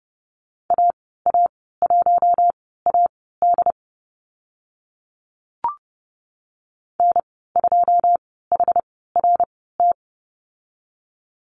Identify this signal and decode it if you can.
Morse code, heard as a keyed tone, about 30 wpm, spaced as for 17 wpm, 690 Hz. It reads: AA1AD N2HRT
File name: Lesson 15g - Number 5 - Call Signs 30wpm